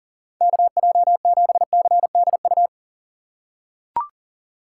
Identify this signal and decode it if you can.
Morse code, heard as a keyed tone, about 40 wpm, 700 Hz. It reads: KJ7CDU
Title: USA Call Signs - 6 Characters - 3x Character Spacing - 40wpm